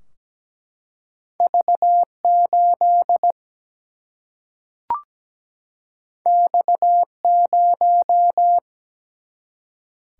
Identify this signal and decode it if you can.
Morse code, heard as a keyed tone, about 17 wpm, 700 Hz. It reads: V 8 X 0
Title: Single-Letter Single-Number 17wpm